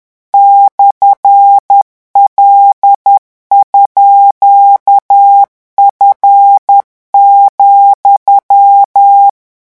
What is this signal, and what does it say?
Morse code, heard as a keyed tone, about 10 wpm, 800 Hz. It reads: /L_F,
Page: REVISION 18 - 10 mots minute